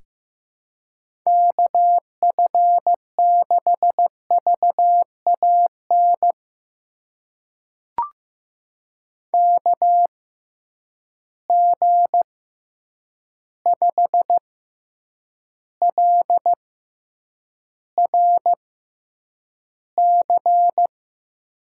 Morse code, heard as a keyed tone, about 16 wpm, spaced as for 8 wpm, 700 Hz. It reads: KF6VAN KG5LRC
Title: USA Call Signs - 6 Characters - 6x Character Spacing - 15wpm